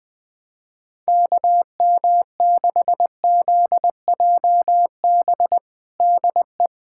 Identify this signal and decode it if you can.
Morse code, heard as a keyed tone, about 20 wpm, 700 Hz. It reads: KM6ZJB DE
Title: USA Callsign DE Callsign 20wpm